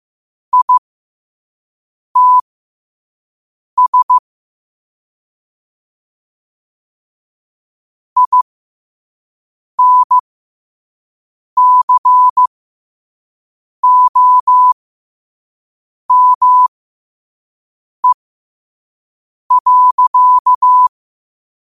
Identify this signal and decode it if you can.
Morse code, heard as a keyed tone, about 15 wpm, spaced as for 5 wpm, 1000 Hz. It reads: ITS INCOME.
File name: qotdfiles / 5wpm